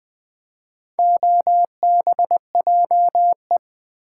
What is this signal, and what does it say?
Morse code, heard as a keyed tone, about 20 wpm, 690 Hz. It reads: OBJE
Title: Build A Word - 2 Letters At A Time - 6-Letter Words - 20wpm